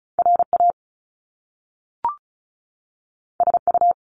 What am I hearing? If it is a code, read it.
Morse code, heard as a keyed tone, about 36 wpm, 700 Hz. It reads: RA SU
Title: Top Two Letter Combinations in Words 35wpm